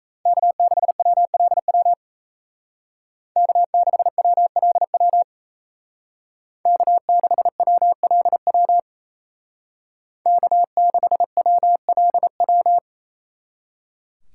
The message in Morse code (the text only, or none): K6WLW K6WLW K6WLW K6WLW